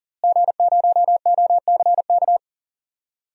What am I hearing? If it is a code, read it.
Morse code, heard as a keyed tone, about 40 wpm, 700 Hz. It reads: G0OCK
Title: DXCC-100 Call Signs - 5 Characters - Repeated 2x - 40wpm